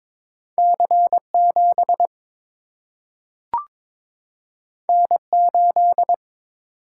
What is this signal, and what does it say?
Morse code, heard as a keyed tone, about 22 wpm, 700 Hz. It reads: C 7 N 8
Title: Single-Letter Single-Number 22wpm